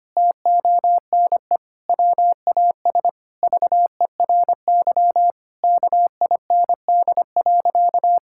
Morse code, heard as a keyed tone, about 25 wpm, 700 Hz. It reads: TONE WAS VERY KIND.
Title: Sentences from Top 1600 Words 25wpm